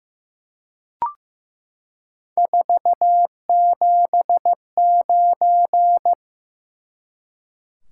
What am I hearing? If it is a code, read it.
Morse code, heard as a keyed tone, about 15 wpm, 700 Hz. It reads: 479
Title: Top 180 QSO Elements 15wpm